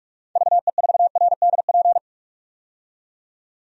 Morse code, heard as a keyed tone, about 43 wpm, 700 Hz. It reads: VE4RDP
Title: DXCC-100 Call Signs - 6 Characters - 45wpm